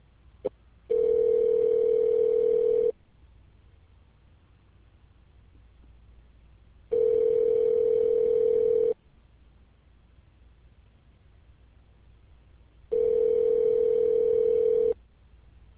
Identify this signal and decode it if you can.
Ringback tone (North America), 440 + 480 Hz.